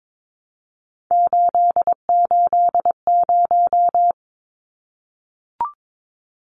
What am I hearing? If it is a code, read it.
Morse code, heard as a keyed tone, about 22 wpm, 700 Hz. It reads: 880
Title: Numbers - 3-Digit 22wpm